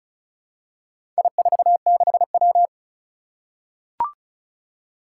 Morse code, heard as a keyed tone, about 36 wpm, 700 Hz. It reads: I46W
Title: Four Challenging Letter-Number Combinations 35wpm